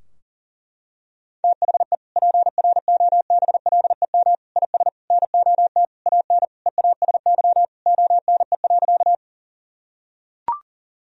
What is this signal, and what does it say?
Morse code, heard as a keyed tone, about 40 wpm, 700 Hz. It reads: THE PROBLEM IS NOT AN EASY ONE.
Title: Sentences from Top 1000 Words - Review Part 2 - 40wpm